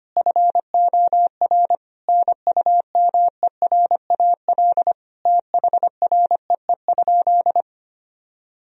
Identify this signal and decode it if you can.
Morse code, heard as a keyed tone, about 25 wpm, 700 Hz. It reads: FOR NUMERAL THREE?
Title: Sentences from Top 1000 Words - Review Part 2 - 25wpm